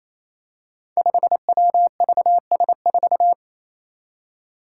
Morse code, heard as a keyed tone, about 28 wpm, 700 Hz. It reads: 5WVS4
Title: Five Challenging Letter-Number Combinations 28wpm